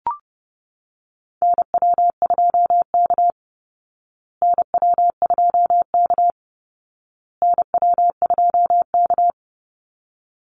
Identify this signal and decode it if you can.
Morse code, heard as a keyed tone, about 30 wpm, 700 Hz. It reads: NW2K NW2K NW2K
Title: Call Signs Repeated 2x - 30wpm